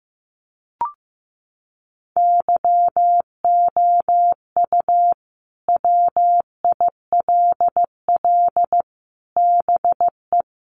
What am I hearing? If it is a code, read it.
Morse code, heard as a keyed tone, about 15 wpm, 700 Hz. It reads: YOU WILL BE
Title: Sets of 5 Words Encore2 15wpm